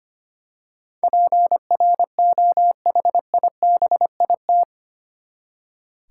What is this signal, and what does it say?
Morse code, heard as a keyed tone, about 25 wpm, 700 Hz. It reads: PROHIBIT
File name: Eight Letter Words - 25wpm